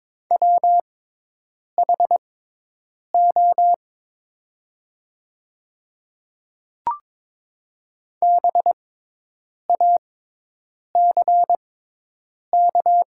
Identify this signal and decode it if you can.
Morse code, heard as a keyed tone, about 22 wpm, spaced as for 8 wpm, 700 Hz. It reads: WHO BACK
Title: 6x Character Spacing - Top 300 Words - 22wpm